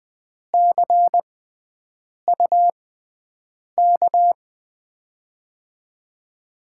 Morse code, heard as a keyed tone, about 21 wpm, spaced as for 7 wpm, 700 Hz. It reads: CUK
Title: USA Call Signs - 6 Characters - 6x Character Spacing - 20wpm